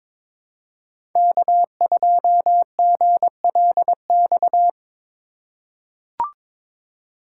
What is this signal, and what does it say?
Morse code, heard as a keyed tone, about 22 wpm, 700 Hz. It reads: K2GLX
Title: USA Call Signs - 2x Character Spacing - 22wpm